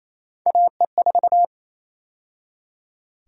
Morse code, heard as a keyed tone, about 28 wpm, 700 Hz. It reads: AE4
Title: USA Call Sign Prefixes - 28wpm